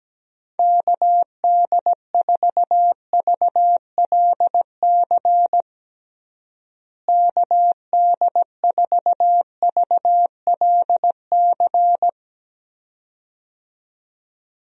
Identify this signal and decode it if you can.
Morse code, heard as a keyed tone, about 17 wpm, 700 Hz. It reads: KD4VLC KD4VLC
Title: USA Call Signs - 6 Characters - Repeated 2x - 17wpm